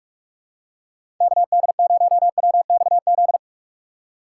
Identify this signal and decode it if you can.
Morse code, heard as a keyed tone, about 44 wpm, 700 Hz. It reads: KD0WXZ
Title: USA Call Signs - 6 Characters - 4x Character Spacing - 45wpm